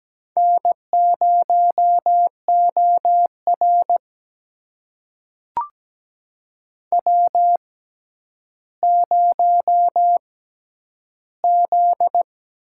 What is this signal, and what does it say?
Morse code, heard as a keyed tone, about 17 wpm, spaced as for 10 wpm, 700 Hz. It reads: N0OR W0Z
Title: USA Call Signs - 4 Characters - 6x Character Spacing - 17wpm